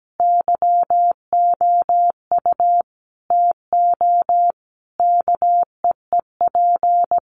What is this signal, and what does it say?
Morse code, heard as a keyed tone, about 17 wpm, 700 Hz. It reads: YOU TO KEEP
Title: Sentences from Top 1300 Words 17wpm